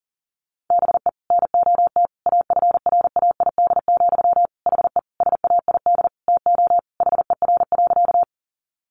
Morse code, heard as a keyed tone, about 37 wpm, 700 Hz. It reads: BE NOT AFRAID, HE SAID TO HER.
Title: Sentences from Top 1500 Words 40wpm